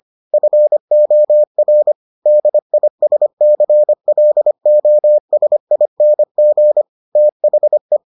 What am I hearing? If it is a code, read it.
Morse code, heard as a keyed tone, about 25 wpm, 600 Hz. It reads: FOR DISCLOSING THE